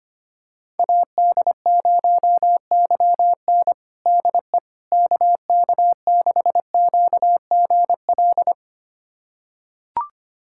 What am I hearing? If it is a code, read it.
Morse code, heard as a keyed tone, about 25 wpm, 700 Hz. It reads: AD0YN DE KK6QGL
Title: USA Callsign DE Callsign 25wpm